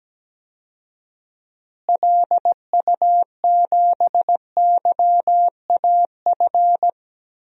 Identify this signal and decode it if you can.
Morse code, heard as a keyed tone, about 17 wpm, 700 Hz. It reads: LU7YAF
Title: DXCC-100 Call Signs - 6 Characters - Repeated 3x - 17wpm